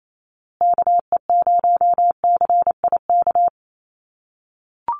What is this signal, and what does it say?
Morse code, heard as a keyed tone, about 28 wpm, 700 Hz. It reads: KE0CIK
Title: USA Call Signs Repeated 3x 28wpm